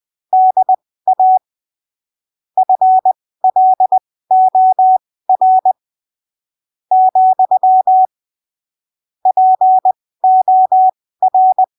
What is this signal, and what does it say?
Morse code, heard as a keyed tone, about 20 wpm, spaced as for 15 wpm, 750 Hz. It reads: DA FLOR , POR
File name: arquivos / texto / texto 20 20ppm